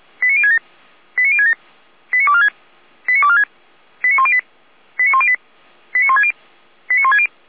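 Five-tone selective call (ZVEI), heard as quick runs of five tones, tones of about 70 ms. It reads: 89067 89067 89246 89246 89108 89108 89160 89160
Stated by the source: Received on an SDR.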